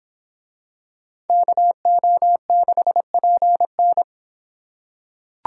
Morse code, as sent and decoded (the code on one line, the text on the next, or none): -.- --- -.... .--. -.
KO6PN